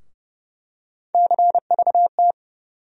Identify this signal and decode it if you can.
Morse code, heard as a keyed tone, about 31 wpm, 700 Hz. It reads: CVT
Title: USA Call Sign Postfixes - 3 Letters 30wpm